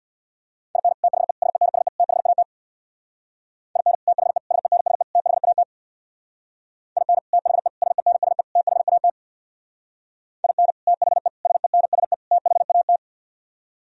Morse code, heard as a keyed tone, about 64 wpm, 700 Hz. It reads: IN THE SENSE THAT IN THE SENSE THAT IN THE SENSE THAT IN THE SENSE THAT